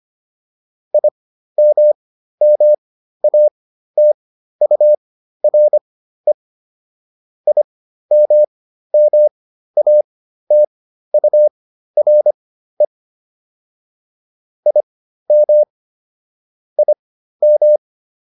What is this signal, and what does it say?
Morse code, heard as a keyed tone, about 25 wpm, spaced as for 13 wpm, 600 Hz. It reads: IMMATURE IMMATURE IM IM